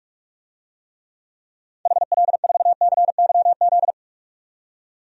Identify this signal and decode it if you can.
Morse code, heard as a keyed tone, about 46 wpm, 700 Hz. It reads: HL4CYZ
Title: DXCC-100 Call Signs - 6 Characters - 45wpm